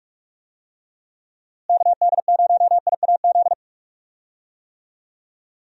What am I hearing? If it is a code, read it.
Morse code, heard as a keyed tone, about 46 wpm, 700 Hz. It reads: KD0IAZ
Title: USA Call Signs - 6 Characters - Repeated 2x - 45wpm